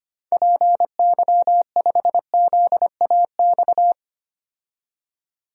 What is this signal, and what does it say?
Morse code, heard as a keyed tone, about 25 wpm, 700 Hz. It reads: PY5ZAX
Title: DXCC-100 Call Signs - 6 Characters - 25wpm